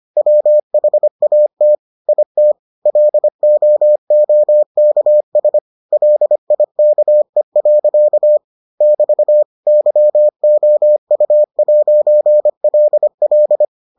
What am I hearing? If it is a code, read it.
Morse code, heard as a keyed tone, about 25 wpm, 600 Hz. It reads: WHAT IT LOOKS LIKE. <BT> YOU'LL